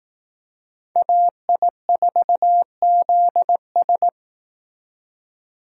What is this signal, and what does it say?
Morse code, heard as a keyed tone, about 18 wpm, 700 Hz. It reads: AI4ZS
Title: Speed Racing - USA 2x2 Call Signs 15wpm